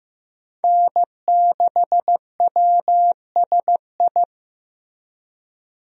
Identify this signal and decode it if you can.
Morse code, heard as a keyed tone, about 15 wpm, 700 Hz. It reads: N6WSI
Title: USA Call Signs - 5 Characters - Repeated 2x - 15wpm